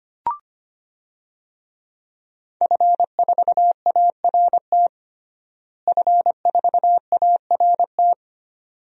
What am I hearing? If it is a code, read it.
Morse code, heard as a keyed tone, about 25 wpm, 700 Hz. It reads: F4ART F4ART
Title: DXCC-100 Call Signs - 5 Characters - Repeated 2x - 25wpm